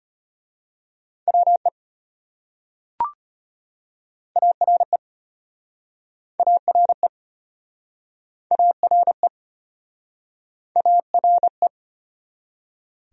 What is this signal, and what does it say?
Morse code, heard as a keyed tone, about 34 wpm, 700 Hz. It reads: WE ARE ARE ARE ARE